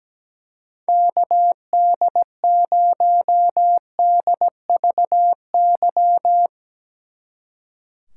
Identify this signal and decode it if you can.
Morse code, heard as a keyed tone, about 18 wpm, 700 Hz. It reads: KD0DVY
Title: Speed Racing - USA 2x3 Call Signs 17wpm